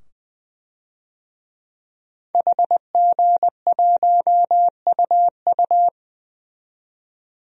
Morse code, heard as a keyed tone, about 21 wpm, 700 Hz. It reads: HG1UU
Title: DXCC-100 Call Signs - 5 Characters - 20wpm